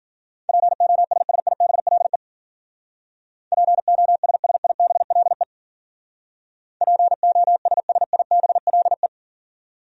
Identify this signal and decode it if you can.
Morse code, heard as a keyed tone, about 50 wpm, 700 Hz. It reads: POSSIBLE POSSIBLE POSSIBLE
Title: Speed-Racing Top 500 Words 40wpm